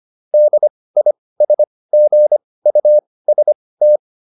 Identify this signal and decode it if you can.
Morse code, heard as a keyed tone, about 25 wpm, spaced as for 18 wpm, 600 Hz. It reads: DISGUST